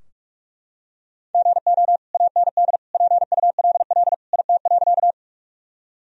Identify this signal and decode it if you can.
Morse code, heard as a keyed tone, about 42 wpm, 700 Hz. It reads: GO AND PULL IT.
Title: Sentences from Top 1000 Words - Review Part 2 - 45wpm